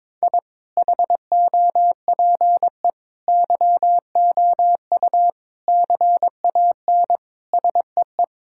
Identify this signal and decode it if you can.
Morse code, heard as a keyed tone, about 22 wpm, 700 Hz. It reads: I HOPE YOU CAN SEE
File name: Sets of 9 Words - 22wpm